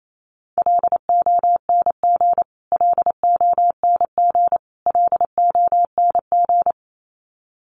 Morse code, heard as a keyed tone, about 28 wpm, 700 Hz. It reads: LONG LONG LONG
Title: Top 300 Words Repeated 28wpm